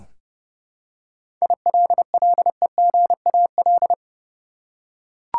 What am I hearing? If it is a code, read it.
Morse code, heard as a keyed tone, about 30 wpm, 700 Hz. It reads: ILLEGAL
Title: Seven Letter Words - 30wpm